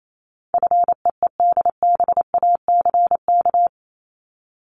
Morse code, heard as a keyed tone, about 28 wpm, 700 Hz. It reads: FEEDBACK
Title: Top 1501-1750 Words 28wpm